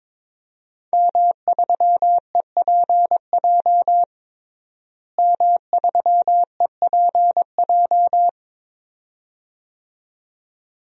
Morse code, heard as a keyed tone, about 22 wpm, 700 Hz. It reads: M3EPJ M3EPJ
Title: DXCC-100 Call Signs - 5 Characters - Repeated 2x - 22wpm